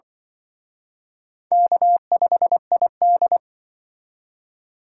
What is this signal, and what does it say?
Morse code, heard as a keyed tone, about 24 wpm, 700 Hz. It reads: K5ID
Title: Speed Racing - USA Call Signs 20wpm